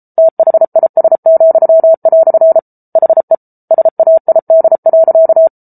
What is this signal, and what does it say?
Morse code, heard as a keyed tone, about 34 wpm, 650 Hz. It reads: THIS," HE SAID.